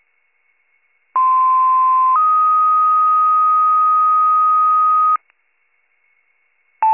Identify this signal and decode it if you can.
Two-tone page, heard as a one-second tone, then a three-second tone, about 1040 then 1250 Hz. It is heard from 1.2 to 5.2 s.